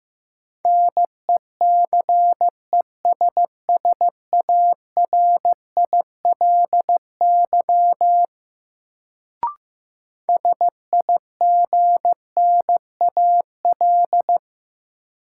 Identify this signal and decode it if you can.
Morse code, heard as a keyed tone, about 15 wpm, 700 Hz. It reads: NECESSARILY SIGNAL
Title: Sentences from Top 1400 Words - New Word Drill - 15wpm